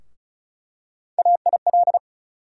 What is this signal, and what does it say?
Morse code, heard as a keyed tone, about 36 wpm, 700 Hz. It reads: AIL
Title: USA Call Sign Postfixes - 3 Letters 35wpm